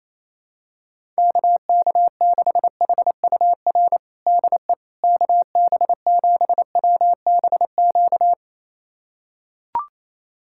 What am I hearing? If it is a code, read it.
Morse code, heard as a keyed tone, about 28 wpm, 700 Hz. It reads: KK6HUR DE KB7WBQ
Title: USA Callsign DE Callsign - 6 Characters - 28wpm